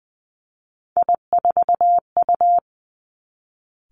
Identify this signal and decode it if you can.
Morse code, heard as a keyed tone, about 20 wpm, 700 Hz. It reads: I4U
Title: Three Challenging Letter-Number Combinations 20wpm